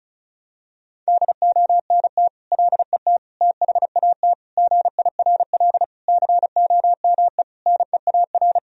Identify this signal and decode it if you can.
Morse code, heard as a keyed tone, about 34 wpm, 700 Hz. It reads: DONT LET THAT GIRL COME NEAR
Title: Sentences from Top 1000 Words - Review Part 1 - 35wpm